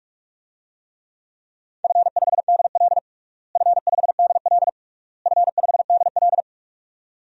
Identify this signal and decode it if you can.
Morse code, heard as a keyed tone, about 41 wpm, 700 Hz. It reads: F5DL F5DL F5DL
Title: DXCC-100 Call Signs - 4 Characters - Repeated 3x - 45wpm